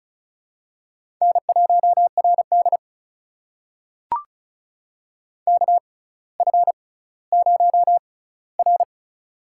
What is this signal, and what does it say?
Morse code, heard as a keyed tone, about 36 wpm, spaced as for 18 wpm, 700 Hz. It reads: N1RD KF0R
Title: USA Call Signs - 4 Characters - 6x Character Spacing - 35wpm